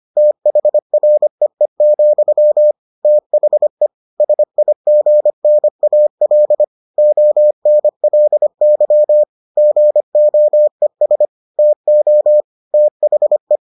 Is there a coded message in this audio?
Yes: Morse code from 0.2 to 13.6 s, heard as a keyed tone, about 25 wpm, 600 Hz. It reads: THREE, THE SIGNAL ONLY GOES TO THE